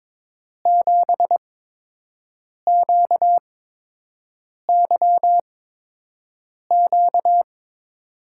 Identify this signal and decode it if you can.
Morse code, heard as a keyed tone, about 22 wpm, 700 Hz. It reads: 7 Q Y Q
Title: USA Call Signs - 6 Characters - 8x Character Spacing - 22wpm